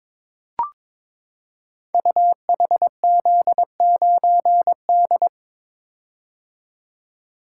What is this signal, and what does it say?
Morse code, heard as a keyed tone, about 23 wpm, 700 Hz. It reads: UHZ9D
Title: Code Groups - 5 Letter:Numbers - 22wpm